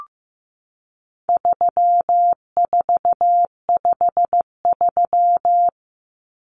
Morse code, heard as a keyed tone, about 15 wpm, 700 Hz. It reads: 3453
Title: Numbers - 4-Digit 15wpm